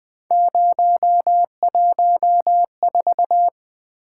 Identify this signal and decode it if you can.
Morse code, heard as a keyed tone, about 20 wpm, 700 Hz. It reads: 014
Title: Three Letter-Number Combinations - 20wpm